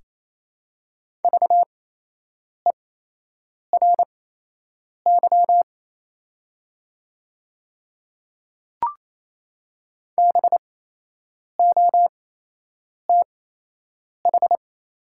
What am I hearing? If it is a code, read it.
Morse code, heard as a keyed tone, about 28 wpm, spaced as for 8 wpm, 700 Hz. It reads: VERY BOTH
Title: 8x Character Spacing - Top 300 Words - 28wpm